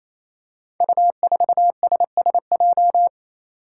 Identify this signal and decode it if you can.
Morse code, heard as a keyed tone, about 27 wpm, 700 Hz. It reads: U4SSJ
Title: Five Challenging Letter-Number Combinations 28wpm